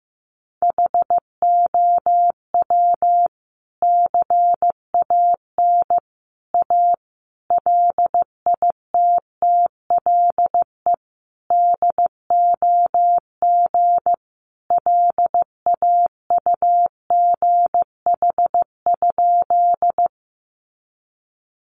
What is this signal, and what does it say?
Morse code, heard as a keyed tone, about 15 wpm, 700 Hz. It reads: HOW CAN A LITTLE DOG LAUGH?
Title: Sentences from Top 500 Words 15wpm